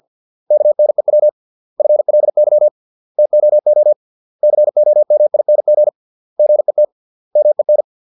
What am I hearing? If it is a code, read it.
Morse code, heard as a keyed tone, about 50 wpm, 600 Hz. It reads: KNEW FLY TOO COMING GET MEN